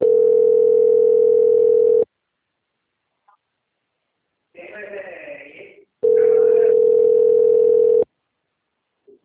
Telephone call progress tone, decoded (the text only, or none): ringback tone